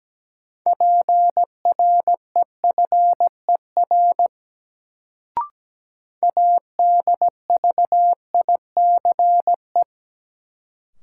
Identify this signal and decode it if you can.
Morse code, heard as a keyed tone, about 17 wpm, 700 Hz. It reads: PREFER ADVICE
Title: Sentences from Top 1100 Words - New Word Drill Alt - 17wpm